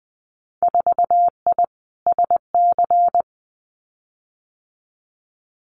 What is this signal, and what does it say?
Morse code, heard as a keyed tone, about 20 wpm, 700 Hz. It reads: 4I SC
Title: Winter Field Day Exchanges 20wpm